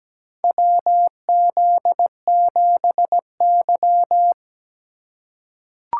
Morse code, heard as a keyed tone, about 17 wpm, 700 Hz. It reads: WZ7Y
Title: USA Call Signs - 4x Character Spacing - 17wpm